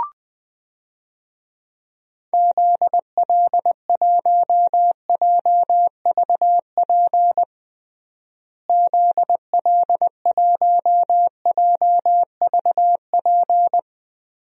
Morse code, heard as a keyed tone, about 20 wpm, 700 Hz. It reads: ZL1JVP ZL1JVP
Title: DXCC-100 Call Signs - 6 Characters - Repeated 3x - 20wpm